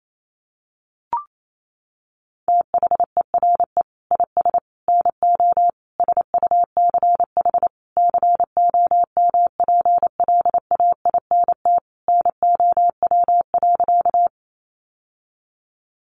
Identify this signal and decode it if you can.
Morse code, heard as a keyed tone, about 28 wpm, 700 Hz. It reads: THERE IS NO SUCH COMPLAINT NOW.
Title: Sentences from Top 1400 Words 28wpm